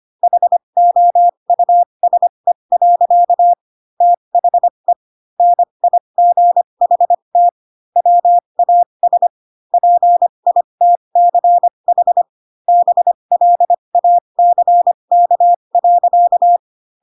Morse code, heard as a keyed tone, about 25 wpm, spaced as for 22 wpm, 700 Hz. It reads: HOUSE. THE NIGHT WAS PITCH BLACK.
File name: Pinocchio / Pinocchio ch6 18 25 ik1hge